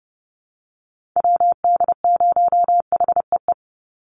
Morse code, heard as a keyed tone, about 30 wpm, 700 Hz. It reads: WD0HEE
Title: USA Call Signs 30wpm